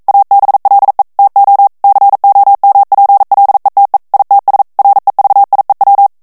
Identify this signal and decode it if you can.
Morse code, heard as a keyed tone, about 42 wpm, 800 Hz. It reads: ABLE TO COMPLETE ITS REVIEW